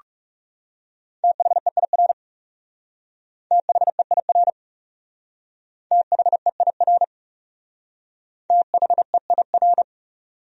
Morse code, heard as a keyed tone, about 39 wpm, 700 Hz. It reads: THEIR THEIR THEIR THEIR